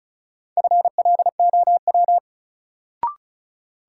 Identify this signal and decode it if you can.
Morse code, heard as a keyed tone, about 34 wpm, 700 Hz. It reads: FLOW
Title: Sentences from Top 1200 Words - New Word Drill - 35wpm